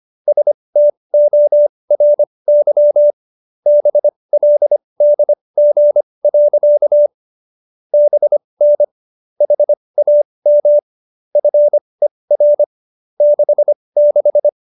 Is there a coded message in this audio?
Yes: Morse code from 0.3 to 14.5 s, heard as a keyed tone, about 25 wpm, spaced as for 20 wpm, 590 Hz. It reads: STORY BLDG. BN HAM FER 66